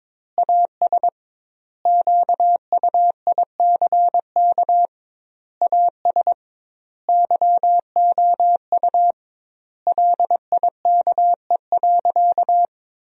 Morse code, heard as a keyed tone, about 22 wpm, 700 Hz. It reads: AS QUICK AS YOU LIKE.
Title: 2x Word-Spacing - Sentences from Top 500 Words - 22wpm